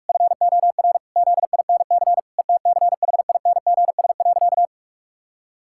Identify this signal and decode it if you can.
Morse code, heard as a keyed tone, about 46 wpm, 700 Hz. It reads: FOR ZINC ETCHINGS.
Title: Pangram Challenge v2 - 45wpm